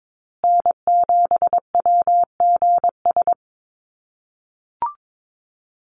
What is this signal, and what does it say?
Morse code, heard as a keyed tone, about 22 wpm, 700 Hz. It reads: N7WGS
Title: USA Call Signs Repeated 3x 22wpm